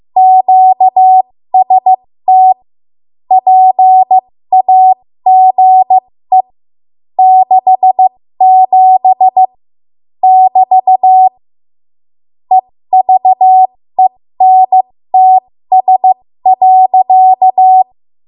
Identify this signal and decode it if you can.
Morse code, heard as a keyed tone, about 15 wpm, spaced as for 13 wpm, 750 Hz. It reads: QST PAGE 67 <BT> EVENTS.
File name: Morse / 13WPM